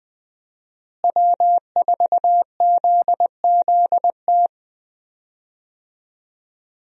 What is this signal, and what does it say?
Morse code, heard as a keyed tone, about 20 wpm, 700 Hz. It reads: W4ZZT
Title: USA Call Signs - 5 Characters - Repeated 3x - 20wpm